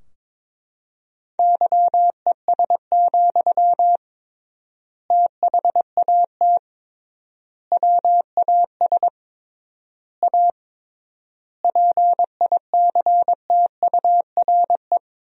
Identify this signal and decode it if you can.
Morse code, heard as a keyed tone, about 22 wpm, 700 Hz. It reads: YES, THAT WAS A PICTURE
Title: 3x Word-Spacing - Sentences from Top 500 Words - 22wpm